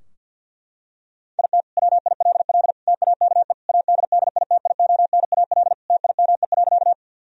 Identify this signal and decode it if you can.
Morse code, heard as a keyed tone, about 50 wpm, 700 Hz. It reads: IT WILL TAKE ADDITIONAL TIME.
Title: Sentences from Top 1000 Words - Review Part 4 - 50wpm